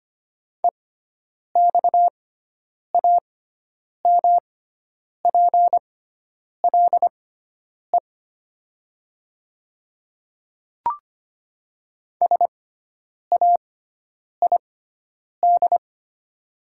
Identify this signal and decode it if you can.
Morse code, heard as a keyed tone, about 25 wpm, spaced as for 9 wpm, 700 Hz. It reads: EXAMPLE SAID
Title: 6x Character Spacing - Top 300 Words - 25wpm